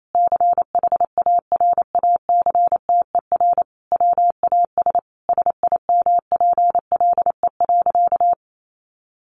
Morse code, heard as a keyed tone, about 28 wpm, 700 Hz. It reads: CHARACTER WAS SIMPLE.
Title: Sentences from Top 1200 Words 28wpm